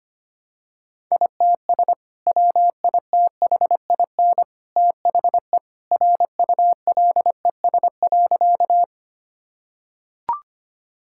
Morse code, heard as a keyed tone, about 25 wpm, 700 Hz. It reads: ITS WITHIN THE RULES.
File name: practice / 25wpm / Sentences with 2-Word Phrases - 25wpm